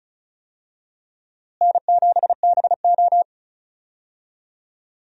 Morse code, heard as a keyed tone, about 36 wpm, 700 Hz. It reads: N7BO
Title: USA Call Signs - 4 Characters - Repeated 2x - 35wpm